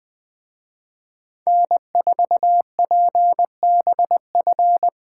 Morse code, heard as a keyed tone, about 21 wpm, 700 Hz. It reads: N4PBF
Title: Speed Racing - USA 1x3 Call Signs 17wpm